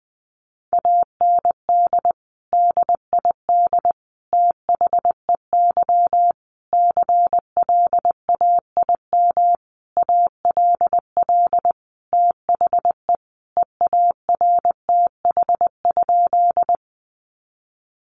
Morse code, heard as a keyed tone, about 20 wpm, 700 Hz. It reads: AND DID THEY CLAIM ALL THE EARTH?
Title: Sentences from Top 1500 Words 20wpm